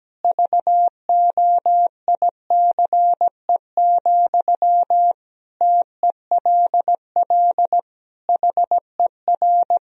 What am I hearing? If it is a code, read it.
Morse code, heard as a keyed tone, about 17 wpm, 700 Hz. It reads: VOICE, TELL HER
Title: Sentences from Top 1000 Words - Review Part 1 - 17wpm